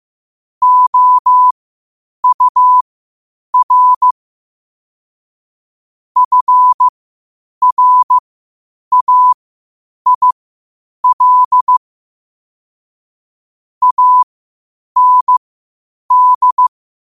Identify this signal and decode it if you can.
Morse code, heard as a keyed tone, about 15 wpm, spaced as for 8 wpm, 1000 Hz. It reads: OUR FRAIL AND
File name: qotdfiles / 8wpm